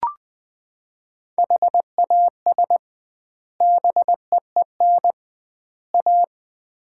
Morse code, heard as a keyed tone, about 21 wpm, 700 Hz. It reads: HAS BEEN A
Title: 2x Word-Spacing - Sets of 3 Words - 20wpm